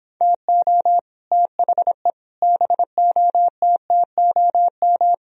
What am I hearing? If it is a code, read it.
Morse code, heard as a keyed tone, about 26 wpm, 700 Hz. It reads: TO THE BOTTOM
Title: Speed-Racing - Sets of 5 Words 22wpm